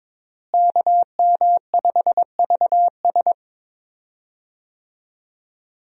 Morse code, heard as a keyed tone, about 22 wpm, 700 Hz. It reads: KM5VS
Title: USA Call Signs - 5 Characters - Repeated 3x - 22wpm